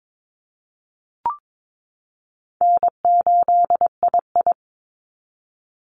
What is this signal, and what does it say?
Morse code, heard as a keyed tone, about 22 wpm, 700 Hz. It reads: N8II
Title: Easy Call Signs Repeated 2x - 22wpm